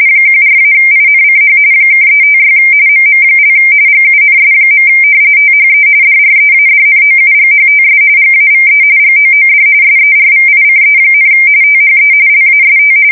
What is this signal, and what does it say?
RTTY, about 45.45 baud. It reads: 14-INCH (356 MM) GUNS. THE DREADNOUGHTS GAVE GOOD SERVICE, THE LAST
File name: fsk170 / battleships